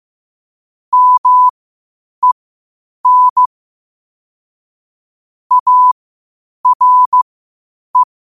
Morse code, heard as a keyed tone, about 15 wpm, spaced as for 8 wpm, 1000 Hz. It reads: MEN ARE
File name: qotdfiles / 8wpm